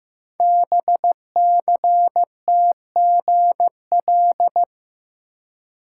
Morse code, heard as a keyed tone, about 15 wpm, 700 Hz. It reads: BCTGL
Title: Code Groups - 5 Letters - 15wpm